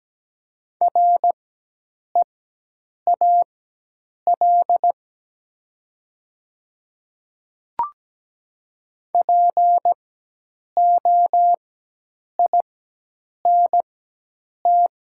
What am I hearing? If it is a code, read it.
Morse code, heard as a keyed tone, about 17 wpm, spaced as for 8 wpm, 700 Hz. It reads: REAL POINT
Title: 4x Character Spacing - Top 500 Words - 17wpm